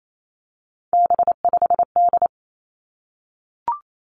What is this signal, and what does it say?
Morse code, heard as a keyed tone, about 28 wpm, 700 Hz. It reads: B5D
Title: Three Challenging Letter-Number Combinations 28wpm